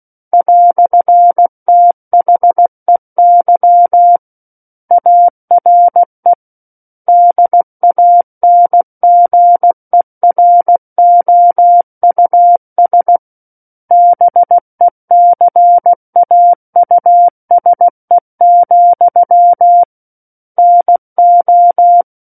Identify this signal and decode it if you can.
Morse code, heard as a keyed tone, about 16 wpm, 700 Hz. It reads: "THEY ARE DANGEROUS BECAUSE, NO
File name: Book One / 07-Chapter 7 - 16 WPM